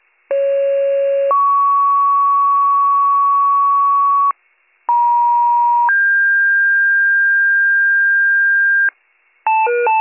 Two-tone page, heard as a one-second tone, then a three-second tone, about 550 then 1120 Hz, about 950 then 1600 Hz, with noise. It is heard from 0.3 to 8.9 s.